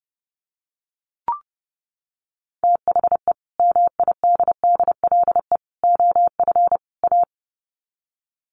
Morse code, heard as a keyed tone, about 30 wpm, 700 Hz. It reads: THE MIDDLE OF A
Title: Sets of 4 Words 30wpm